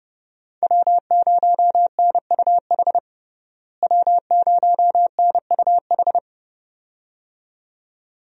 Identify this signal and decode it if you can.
Morse code, heard as a keyed tone, about 30 wpm, 700 Hz. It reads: W0NUH W0NUH
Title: USA Call Signs - 5 Characters - Repeated 2x - 30wpm